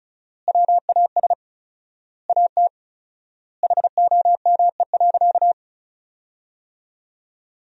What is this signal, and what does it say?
Morse code, heard as a keyed tone, about 36 wpm, 700 Hz. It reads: WAS AT HOME.
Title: 4x Word-Spacing - Sentences from Top 400 Words - 35wpm